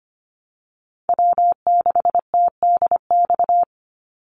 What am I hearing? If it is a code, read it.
Morse code, heard as a keyed tone, about 25 wpm, 700 Hz. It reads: W6TDX
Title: NA SOTA Chaser Callsigns 25wpm